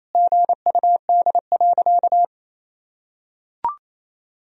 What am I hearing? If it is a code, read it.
Morse code, heard as a keyed tone, about 28 wpm, 700 Hz. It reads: GUD.
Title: Sentences from Top 200 Words and Abbreviations 28wpm